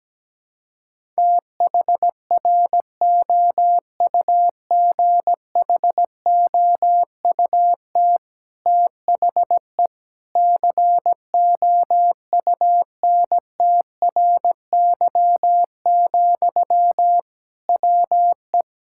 Morse code, heard as a keyed tone, about 17 wpm, 700 Hz. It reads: THROUGHOUT THE COUNTRY, WE
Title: Sentences from Top 1000 Words - Review Part 5 - 17wpm